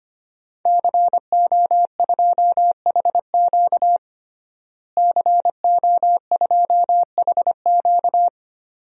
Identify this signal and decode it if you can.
Morse code, heard as a keyed tone, about 25 wpm, 700 Hz. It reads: CO2HQ CO2HQ
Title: DXCC-100 Call Signs - 5 Characters - Repeated 2x - 25wpm